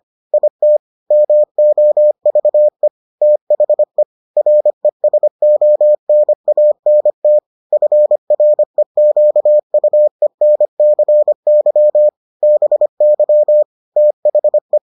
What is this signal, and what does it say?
Morse code, heard as a keyed tone, about 25 wpm, 600 Hz. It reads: IT MOVE THE RESONANT FREQUENCY BY THE